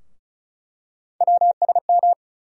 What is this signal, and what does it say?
Morse code, heard as a keyed tone, about 36 wpm, 700 Hz. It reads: WSM
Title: USA Call Sign Postfixes - 3 Letters 35wpm